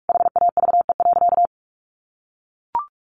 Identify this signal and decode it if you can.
Morse code, heard as a keyed tone, about 44 wpm, 700 Hz. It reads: HAVE.
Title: Sentences from Top 1400 Words 45wpm